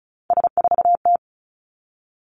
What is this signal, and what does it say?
Morse code, heard as a keyed tone, about 35 wpm, 700 Hz. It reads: S4T
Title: Three Challenging Letter-Number Combinations 35wpm